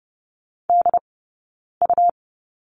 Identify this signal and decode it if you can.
Morse code, heard as a keyed tone, about 30 wpm, spaced as for 9 wpm, 690 Hz. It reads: DU
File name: Lesson 18d - Letter m - Two Characters 30wpm